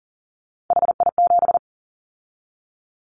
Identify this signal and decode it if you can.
Morse code, heard as a keyed tone, about 40 wpm, 700 Hz. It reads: HI7
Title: Callsign Prefixes for Top 100 DXCC - 40wpm